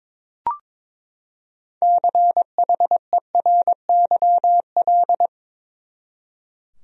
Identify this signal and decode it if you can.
Morse code, heard as a keyed tone, about 22 wpm, 700 Hz. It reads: CHERYL
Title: USA Common Names - 22wpm